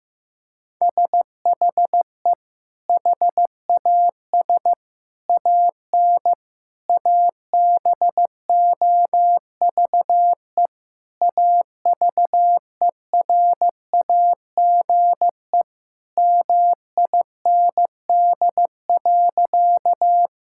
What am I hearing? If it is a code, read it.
Morse code, heard as a keyed tone, about 16 wpm, 700 Hz. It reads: SHE HAS AN ABOVE AVERAGE MIND.
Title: Sentences from Top 1000 Words - Review Part 6 - 15wpm